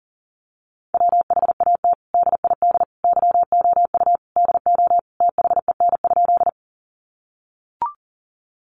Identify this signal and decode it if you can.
Morse code, heard as a keyed tone, about 40 wpm, 700 Hz. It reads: WHAT DID YOU DO THEN?
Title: Sentences from Top 100 Words 40wpm